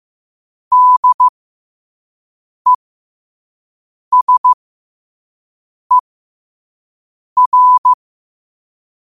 Morse code, heard as a keyed tone, about 15 wpm, spaced as for 5 wpm, 1000 Hz. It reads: DESER
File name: qotdfiles / 5wpm